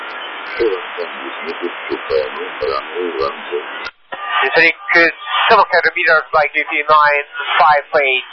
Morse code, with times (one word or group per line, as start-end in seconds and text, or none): none